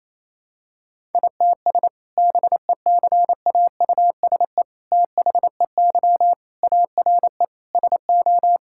Morse code, heard as a keyed tone, about 28 wpm, 700 Hz. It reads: ITS BECAUSE THEY ARE SO
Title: Sentences from Top 1000 Words - Review Part 6 - 28wpm